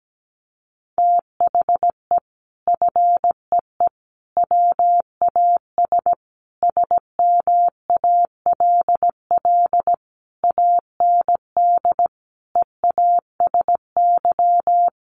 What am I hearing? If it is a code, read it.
Morse code, heard as a keyed tone, about 17 wpm, 700 Hz. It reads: THE FEE WAS SMALL AND EASY
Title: Sentences from Top 1500 Words 17wpm